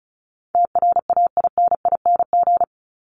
Morse code, heard as a keyed tone, about 34 wpm, 700 Hz. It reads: TRAINING
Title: Top 501-750 Words 35wpm